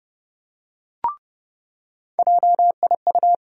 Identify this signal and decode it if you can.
Morse code, heard as a keyed tone, about 30 wpm, 700 Hz. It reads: JIU
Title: USA Call Sign Postfixes - 3 Letters 30wpm